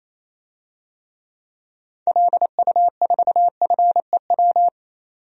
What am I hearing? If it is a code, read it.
Morse code, heard as a keyed tone, about 27 wpm, 700 Hz. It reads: LU4FEW
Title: DXCC-100 Call Signs - 6 Characters - 28wpm